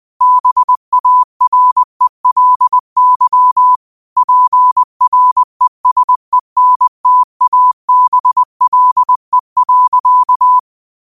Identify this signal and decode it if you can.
Morse code, heard as a keyed tone, about 20 wpm, 1000 Hz. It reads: BARELY PRESENTABLE.
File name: qotdfiles / 20wpm